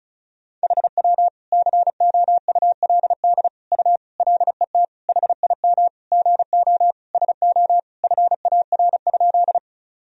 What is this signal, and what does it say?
Morse code, heard as a keyed tone, about 36 wpm, 700 Hz. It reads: HW COULD U LET HIM GO SO FAR?
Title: Sentences from Top 300 Words and Abbreviations 35wpm